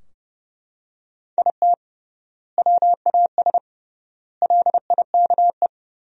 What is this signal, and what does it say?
Morse code, heard as a keyed tone, about 32 wpm, 700 Hz. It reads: IT WAS LIKE
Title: 3x Word-Spacing - Sentences from Top 300 Words - 30wpm